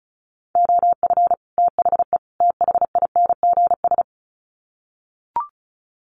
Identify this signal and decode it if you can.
Morse code, heard as a keyed tone, about 34 wpm, 700 Hz. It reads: OF THE THINGS
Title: Sets of 3 Words 35wpm